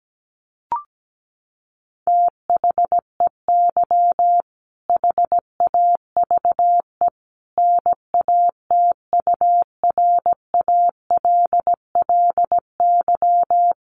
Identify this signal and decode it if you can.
Morse code, heard as a keyed tone, about 17 wpm, 700 Hz. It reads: THEY HAVE NATURALLY
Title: Sentences from Top 1300 Words 17wpm